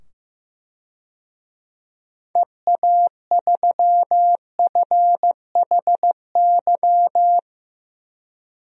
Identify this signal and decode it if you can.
Morse code, heard as a keyed tone, about 15 wpm, 700 Hz. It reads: EA3FHY
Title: DXCC-100 Call Signs - 6 Characters - 15wpm